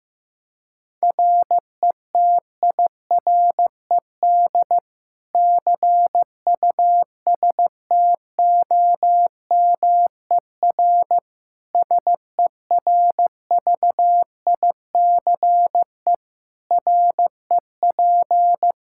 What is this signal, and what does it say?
Morse code, heard as a keyed tone, about 16 wpm, 700 Hz. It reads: RETIRED CUSTOMER SERVICE REP
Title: Practice QSO - Exchange 3 - More Info - 15wpm